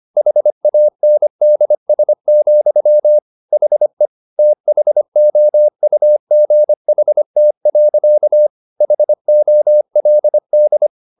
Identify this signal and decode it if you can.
Morse code, heard as a keyed tone, about 25 wpm, 600 Hz. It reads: HANDS, HE THOUGHT. HOLD